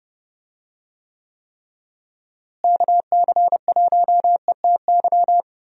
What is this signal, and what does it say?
Morse code, heard as a keyed tone, about 30 wpm, 700 Hz. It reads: KC1ETY
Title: USA Call Signs - 6 Characters - Repeated 3x - 30wpm